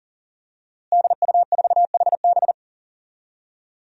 Morse code, heard as a keyed tone, about 37 wpm, 700 Hz. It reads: DU4HB
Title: DXCC-100 Call Signs - 5 Characters - Repeated 3x - 40wpm